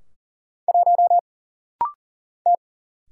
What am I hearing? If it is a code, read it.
Morse code, heard as a keyed tone, about 40 wpm, 700 Hz. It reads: 1 T
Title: Single Letter-Number - Rapid-Fire 40wpm